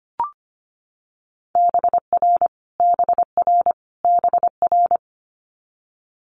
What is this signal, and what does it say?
Morse code, heard as a keyed tone, about 25 wpm, 700 Hz. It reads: BR BR BR
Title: Multi-letter Phonemes Repeated 3x 25wpm